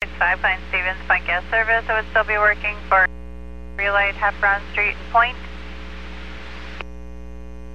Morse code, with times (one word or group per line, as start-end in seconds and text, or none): none